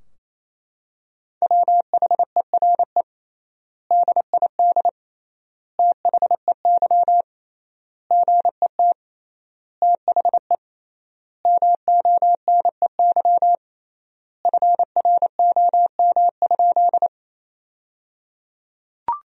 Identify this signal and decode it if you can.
Morse code, heard as a keyed tone, about 28 wpm, 700 Hz. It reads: WHERE DID THEY GET THE MONEY FROM?
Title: 3x Word-Spacing - Sentences from Top 400 Words - 28wpm